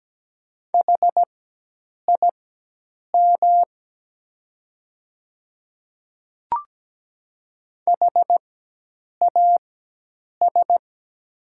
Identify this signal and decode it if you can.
Morse code, heard as a keyed tone, about 17 wpm, spaced as for 8 wpm, 700 Hz. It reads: HIM HAS
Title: 4x Character Spacing - Top 100 Words - 17wpm